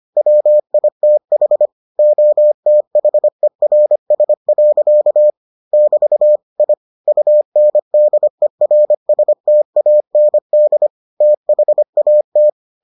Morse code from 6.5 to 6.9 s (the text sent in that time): I